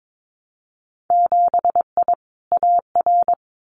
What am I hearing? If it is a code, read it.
Morse code, heard as a keyed tone, about 22 wpm, 700 Hz. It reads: 7I AR
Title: Winter Field Day Exchanges Encore 22wpm